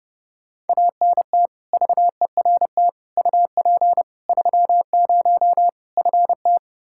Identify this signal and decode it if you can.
Morse code, heard as a keyed tone, about 32 wpm, 700 Hz. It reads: ANT VERT UP 30 FT
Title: K1USN SST QSO Practice 30wpm